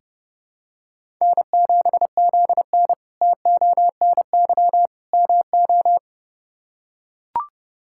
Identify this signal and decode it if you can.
Morse code, heard as a keyed tone, about 30 wpm, 700 Hz. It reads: N7ZN TONY MO
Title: K1USN SST QSO Practice - 30wpm